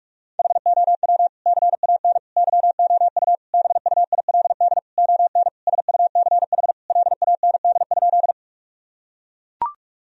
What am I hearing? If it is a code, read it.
Morse code, heard as a keyed tone, about 42 wpm, 700 Hz. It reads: HOW CAN YOU BUILD ON SUCH LAND?
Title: Sentences from Top 1000 Words - Review Part 1 - 45wpm